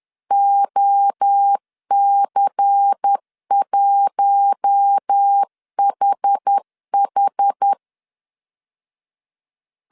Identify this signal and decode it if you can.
Morse code, heard as a keyed tone, about 10 wpm, 800 Hz. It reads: OC1HH